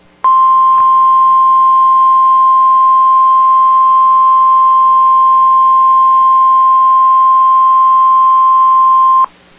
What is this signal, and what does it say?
Emergency Alert System (SAME) tones (the 1050 Hz attention tone). It is heard from 0.2 to 9.3 s.